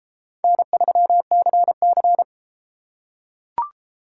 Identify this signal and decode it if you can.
Morse code, heard as a keyed tone, about 32 wpm, 700 Hz. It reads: N3CC